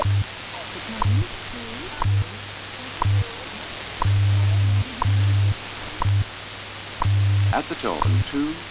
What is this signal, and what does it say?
Radio time signal, with speech.
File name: Radio / SDR Audio / gqrx